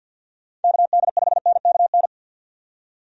Morse code, heard as a keyed tone, about 51 wpm, 700 Hz. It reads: KD5NKN
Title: USA Call Signs - 6 Characters - 50wpm